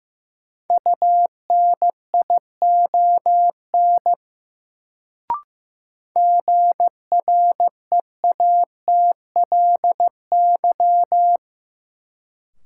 Morse code, heard as a keyed tone, about 15 wpm, 700 Hz. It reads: UNION GREATLY
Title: Sentences from Top 1400 Words - New Word Drill - 15wpm